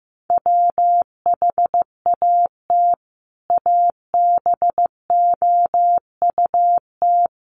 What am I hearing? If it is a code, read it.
Morse code, heard as a keyed tone, about 15 wpm, 700 Hz. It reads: WHAT ABOUT
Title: Sentences from Top 1100 Words 15wpm